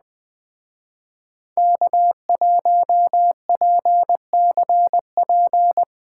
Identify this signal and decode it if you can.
Morse code, heard as a keyed tone, about 20 wpm, 700 Hz. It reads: K1PCP